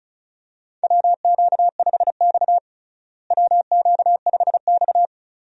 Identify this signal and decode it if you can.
Morse code, heard as a keyed tone, about 36 wpm, 700 Hz. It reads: WQ5X WQ5X
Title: USA Call Signs - 4 Characters - Repeated 2x - 35wpm